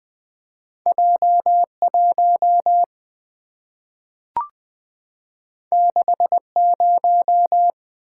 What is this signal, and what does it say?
Morse code, heard as a keyed tone, about 20 wpm, 700 Hz. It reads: J 1 6 0
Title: Two Letter-Number Combinations - 20wpm